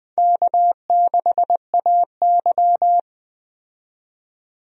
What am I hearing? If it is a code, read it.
Morse code, heard as a keyed tone, about 21 wpm, 700 Hz. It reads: K6AY
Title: USA Call Signs - 4 Characters - 20wpm